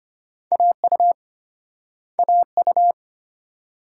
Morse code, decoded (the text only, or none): AU AU